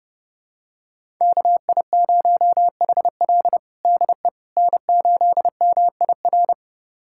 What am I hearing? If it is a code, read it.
Morse code, heard as a keyed tone, about 32 wpm, 700 Hz. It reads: KI0HL DE N8MIR
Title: USA Callsign DE Callsign - 5 Characters - 30wpm